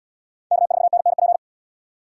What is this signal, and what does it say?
Morse code, heard as a keyed tone, about 68 wpm, 700 Hz. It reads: K4NNP